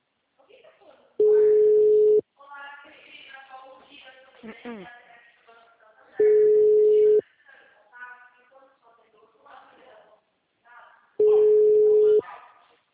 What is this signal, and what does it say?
Ringback tone (Europe, 425 Hz), with speech.
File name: out- -2006- - - .4921